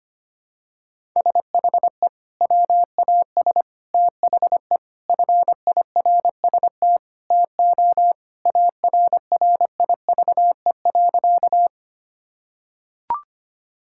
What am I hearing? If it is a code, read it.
Morse code, heard as a keyed tone, about 25 wpm, 700 Hz. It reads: SHE WAS THE FIRST TO ARRIVE.
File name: practice / 25wpm / Sentences with 3-Word Phrases - 25wpm